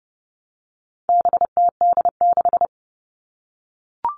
Morse code, heard as a keyed tone, about 30 wpm, 700 Hz. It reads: BTD6
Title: All Cut Numbers - 4-Digit 30wpm